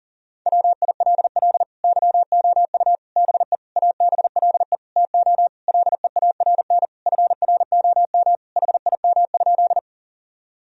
Morse code, heard as a keyed tone, about 40 wpm, 700 Hz. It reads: WILL YOU BE ABLE TO LEARN FROM HIM?
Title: Sentences from Top 1000 Words - Review Part 2 - 40wpm